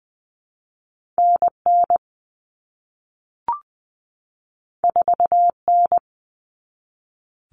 Morse code, heard as a keyed tone, about 20 wpm, 700 Hz. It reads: NN 4N
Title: All Cut Numbers - 2-Digit 20wpm